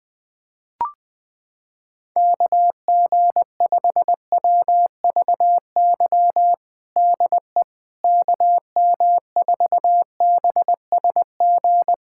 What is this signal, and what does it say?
Morse code, heard as a keyed tone, about 20 wpm, 700 Hz. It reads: KG5WVY DE KM4BSG
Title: USA Callsign DE Callsign - 6 Characters - 20wpm